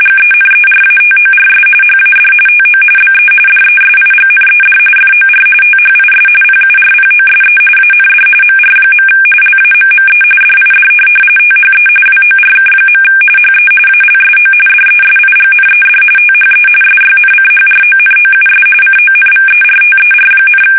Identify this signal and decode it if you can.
RTTY, about 45.45 baud. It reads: GENERATORS HAD TO BE DESIGNED FOR THIS PURPOSE AND MR. LAMME DID THE WORK. HE ALSO DESIGNED THE SYNCHRONOUS CONVERTER,
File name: fsk850 / benjamin-g-lamme